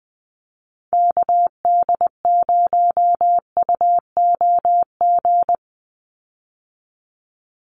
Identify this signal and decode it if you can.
Morse code, heard as a keyed tone, about 20 wpm, 700 Hz. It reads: KD0UOG
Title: USA Call Signs 20wpm